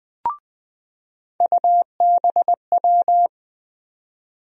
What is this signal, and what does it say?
Morse code, heard as a keyed tone, about 20 wpm, 700 Hz. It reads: UBW
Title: USA Call Sign Postfixes - 3 Letters 20wpm